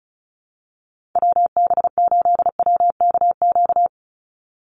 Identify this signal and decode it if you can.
Morse code, heard as a keyed tone, about 34 wpm, 700 Hz. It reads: WB8WKQ
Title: Call Signs - 35wpm